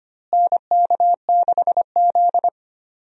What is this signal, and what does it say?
Morse code, heard as a keyed tone, about 25 wpm, 700 Hz. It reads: NK6Z
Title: USA Call Signs - 4 Characters - Repeated 3x - 25wpm